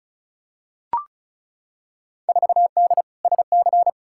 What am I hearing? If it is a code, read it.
Morse code, heard as a keyed tone, about 34 wpm, 700 Hz. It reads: 4D SC
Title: Field Day Exchanges Encore - 35wpm